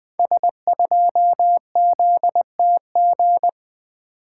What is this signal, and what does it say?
Morse code, heard as a keyed tone, about 20 wpm, 700 Hz. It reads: S2ZTG
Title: Code Groups - 5 Letter:Numbers - 20wpm